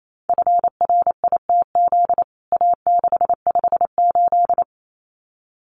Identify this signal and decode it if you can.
Morse code, heard as a keyed tone, about 28 wpm, 700 Hz. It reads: FRITZ A658
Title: CWT Exchange 28wpm